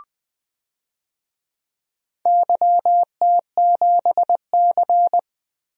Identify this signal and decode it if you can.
Morse code, heard as a keyed tone, about 20 wpm, 700 Hz. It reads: YT7C
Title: DXCC-100 Call Signs - 4 Characters - Repeated 3x - 20wpm